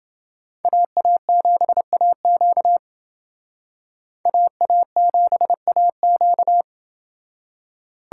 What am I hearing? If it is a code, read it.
Morse code, heard as a keyed tone, about 29 wpm, 700 Hz. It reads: AA7AQ AA7AQ